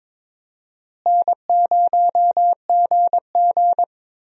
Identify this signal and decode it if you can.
Morse code, heard as a keyed tone, about 22 wpm, 700 Hz. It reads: N0GG
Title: USA Call Signs - 4 Characters - Repeated 3x - 22wpm